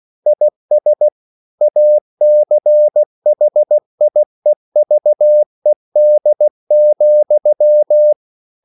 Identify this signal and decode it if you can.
Morse code, heard as a keyed tone, about 16 wpm, 600 Hz. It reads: IS ACHIEVED,
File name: CW / Practice / EnglishQuotations0216